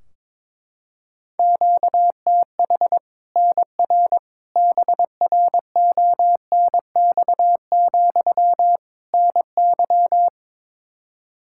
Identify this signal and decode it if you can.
Morse code, heard as a keyed tone, about 22 wpm, 700 Hz. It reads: QTH NR BRONX, NY
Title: Practice QSO - Exchange 1 - Beginning - 22wpm